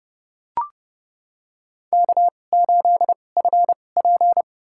Morse code, heard as a keyed tone, about 30 wpm, spaced as for 21 wpm, 700 Hz. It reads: K8FP
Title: USA Call Signs - 4 Characters - 2x Character Spacing - 30wpm